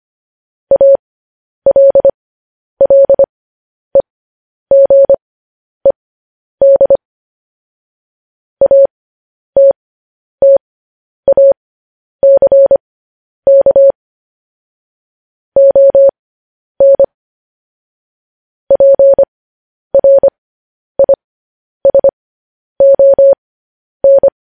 Morse code, heard as a keyed tone, about 25 wpm, spaced as for 10 wpm, 550 Hz. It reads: ALLEGED ATTACK ON PRISON
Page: news-headlines-in-morse-code-at-10-wpm